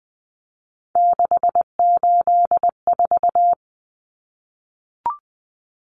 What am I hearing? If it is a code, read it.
Morse code, heard as a keyed tone, about 20 wpm, 700 Hz. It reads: 684
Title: Numbers - 3-Digit 20wpm